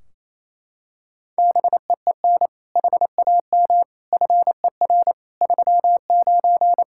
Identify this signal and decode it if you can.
Morse code, heard as a keyed tone, about 27 wpm, 700 Hz. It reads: BEEN HAM FER 39
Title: K1USN SST QSO Practice 28wpm